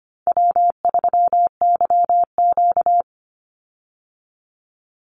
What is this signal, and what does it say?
Morse code, heard as a keyed tone, about 25 wpm, 700 Hz. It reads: W3YQ
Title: Easy Call Signs Repeated 3x - 25wpm